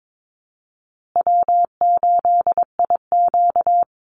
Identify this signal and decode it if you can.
Morse code, heard as a keyed tone, about 22 wpm, 700 Hz. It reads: W8IQ
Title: Call Signs Repeated 3x - 22wpm